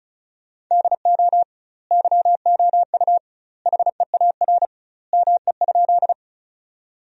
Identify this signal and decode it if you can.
Morse code, heard as a keyed tone, about 34 wpm, 700 Hz. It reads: DO YOU HEAR ME?
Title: 2x Word-Spacing - Sentences from Top 300 Words - 35wpm